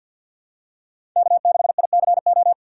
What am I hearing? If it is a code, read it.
Morse code, heard as a keyed tone, about 50 wpm, 700 Hz. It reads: K6ICO
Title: USA Call Signs - 6x Character Spacing - 50wpm